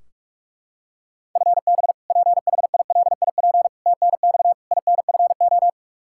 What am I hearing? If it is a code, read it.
Morse code, heard as a keyed tone, about 42 wpm, 700 Hz. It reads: FB PHILIP TNX INFO
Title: Practice QSO - Exchange 3 - More Info - 45wpm